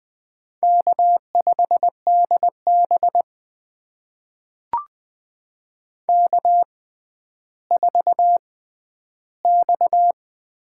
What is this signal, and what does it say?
Morse code, heard as a keyed tone, about 20 wpm, spaced as for 11 wpm, 700 Hz. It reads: K5DB K4X
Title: USA Call Signs - 4 Characters - 6x Character Spacing - 20wpm